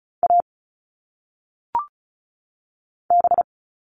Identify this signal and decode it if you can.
Morse code, heard as a keyed tone, about 36 wpm, 700 Hz. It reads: A B
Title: All Cut Numbers - 1-Digit 35wpm